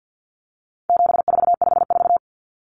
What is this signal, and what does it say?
Morse code, heard as a keyed tone, about 49 wpm, 700 Hz. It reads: 7454
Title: Numbers - 4-Digit 50wpm